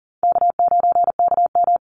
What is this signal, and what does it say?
Morse code, heard as a keyed tone, about 40 wpm, 700 Hz. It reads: K9KM
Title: Call Signs Repeated 2x - 40wpm